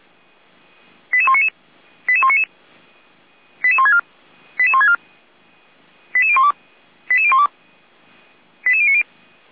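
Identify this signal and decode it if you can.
Five-tone selective call (ZVEI), heard as quick runs of five tones, tones of about 70 ms. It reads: 88199 88199 88164 88164 88012 88012 88090
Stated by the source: ZVEI five-tone selective call received by SDR.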